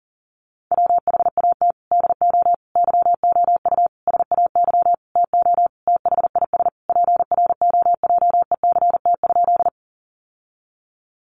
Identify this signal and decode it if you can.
Morse code, heard as a keyed tone, about 37 wpm, 700 Hz. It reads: WHAT DO YOU SAY TO THIS PROJECT?
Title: Sentences from Top 1400 Words 40wpm